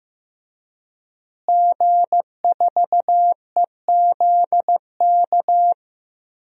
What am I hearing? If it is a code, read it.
Morse code, heard as a keyed tone, about 15 wpm, 700 Hz. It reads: G4EZK
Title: DXCC-100 Call Signs - 5 Characters - Repeated 2x - 15wpm